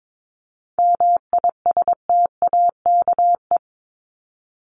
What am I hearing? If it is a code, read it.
Morse code, heard as a keyed tone, about 22 wpm, 700 Hz. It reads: MISTAKE
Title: Top 1501-1750 Words 22wpm